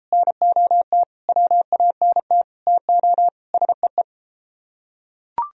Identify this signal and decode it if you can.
Morse code, heard as a keyed tone, about 30 wpm, 700 Hz. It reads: NOT WANT TO SEE